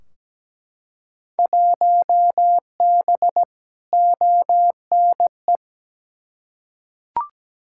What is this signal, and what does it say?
Morse code, heard as a keyed tone, about 17 wpm, 700 Hz. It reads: 1B ONE
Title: Field Day Exchanges - 17wpm